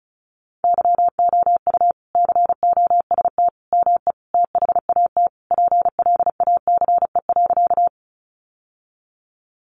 Morse code, heard as a keyed tone, about 34 wpm, 700 Hz. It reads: YOU COST ME THAT PLACE.
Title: Sentences from Top 800 Words 35wpm